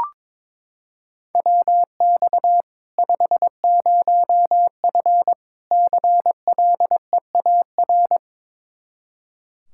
Morse code, heard as a keyed tone, about 22 wpm, 700 Hz. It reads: WX 50F CLEAR
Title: K1USN SST QSO Practice 22wpm